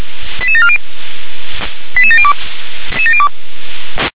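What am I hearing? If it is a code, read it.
Five-tone selective call (ZVEI), heard as quick runs of five tones, tones of about 70 ms. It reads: 88740 88702
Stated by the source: ZVEI five-tone selective call received with an SDR.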